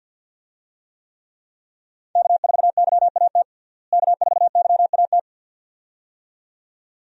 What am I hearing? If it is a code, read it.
Morse code, heard as a keyed tone, about 47 wpm, 700 Hz. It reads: K4YAT K4YAT
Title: USA Call Signs - 5 Characters - Repeated 2x - 50wpm